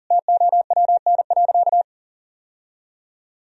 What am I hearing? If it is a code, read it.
Morse code, heard as a keyed tone, about 40 wpm, 690 Hz. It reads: TOWN.
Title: Sentences from Top 1000 Words - Review Part 4 - 40wpm